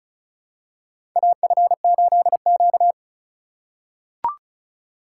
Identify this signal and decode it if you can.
Morse code, heard as a keyed tone, about 36 wpm, 700 Hz. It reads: AF8Q
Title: USA Call Signs - 4x Character Spacing - 35wpm